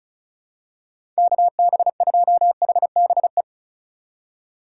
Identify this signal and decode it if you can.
Morse code, heard as a keyed tone, about 36 wpm, 700 Hz. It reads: KB2HBE
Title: USA Call Signs - 6 Characters - 2x Character Spacing - 35wpm